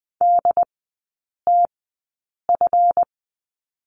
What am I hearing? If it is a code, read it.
Morse code, heard as a keyed tone, about 20 wpm, spaced as for 8 wpm, 690 Hz. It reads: DTF
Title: 145 Lesson 24e - Letter y - Three Characters 20wpm